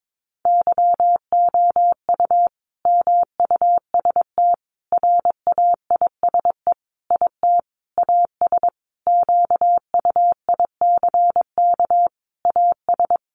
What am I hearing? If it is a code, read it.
Morse code, heard as a keyed tone, about 22 wpm, 700 Hz. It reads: YOU MUST RAISE IT AS QUICK AS
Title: Sentences from Top 1400 Words 22wpm